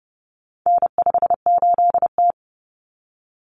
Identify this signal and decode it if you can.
Morse code, heard as a keyed tone, about 30 wpm, 700 Hz. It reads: N58T
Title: Common Cut Numbers - 4-Digit 30wpm